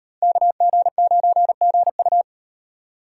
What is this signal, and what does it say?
Morse code, heard as a keyed tone, about 35 wpm, 700 Hz. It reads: KG9GU